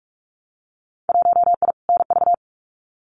Morse code, heard as a keyed tone, about 46 wpm, 700 Hz. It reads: 1I NV
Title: Winter Field Day Exchanges Encore 45wpm